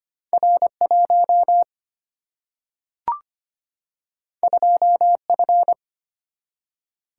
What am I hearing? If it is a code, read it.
Morse code, heard as a keyed tone, about 25 wpm, 700 Hz. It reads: R 1 2 F
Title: Single-Letter Single-Number 25wpm